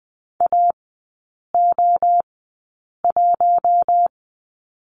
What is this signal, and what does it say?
Morse code, heard as a keyed tone, about 20 wpm, spaced as for 8 wpm, 700 Hz. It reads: AO1
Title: 047 Lesson 9e - Number 4 - Three Characters 20wpm